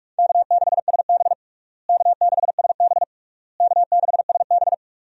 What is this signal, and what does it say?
Morse code, heard as a keyed tone, about 46 wpm, 700 Hz. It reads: K6SB K6SB K6SB
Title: USA Call Signs - 4 Characters - Repeated 3x - 45wpm